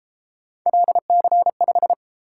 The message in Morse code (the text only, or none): LC5